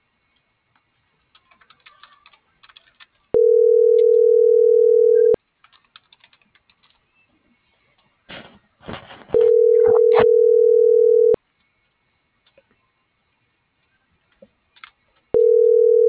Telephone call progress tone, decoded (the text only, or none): ringback tone